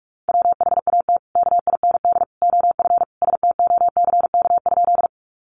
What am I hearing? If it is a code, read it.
Morse code, heard as a keyed tone, about 44 wpm, 700 Hz. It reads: WHAT KIND OF STOCK?
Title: Sentences from Top 800 Words 45wpm